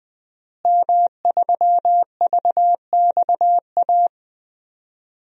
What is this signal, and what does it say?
Morse code, heard as a keyed tone, about 20 wpm, 700 Hz. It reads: M3VXA
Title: DXCC-100 Call Signs - 5 Characters - Repeated 2x - 20wpm